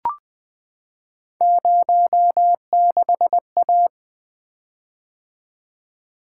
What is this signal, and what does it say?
Morse code, heard as a keyed tone, about 20 wpm, 700 Hz. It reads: 06A
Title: Three Letter-Number Combinations - 20wpm